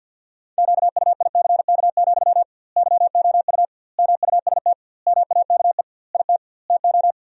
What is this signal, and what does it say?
Morse code, heard as a keyed tone, about 52 wpm, 700 Hz. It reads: QUICK, YOU MUST MAKE IT TO
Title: 2x Word-Spacing - Sentences from Top 500 Words - 50wpm